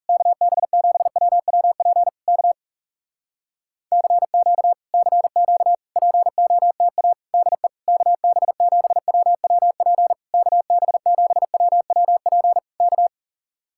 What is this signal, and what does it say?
Morse code, heard as a keyed tone, about 40 wpm, 700 Hz. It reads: KB7WWP K CQ CQ POTA DE KB7WWP KB7WWP K